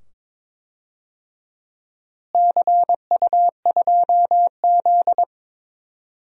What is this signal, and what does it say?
Morse code, heard as a keyed tone, about 22 wpm, 700 Hz. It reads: CU2Z
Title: DXCC-100 Call Signs - 4 Characters - Repeated 2x - 22wpm